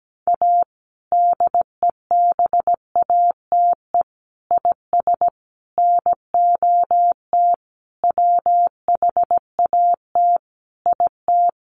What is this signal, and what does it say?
Morse code, heard as a keyed tone, about 17 wpm, 700 Hz. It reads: A DEBATE IS NOT WHAT IT
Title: Sentences from Top 1100 Words 17wpm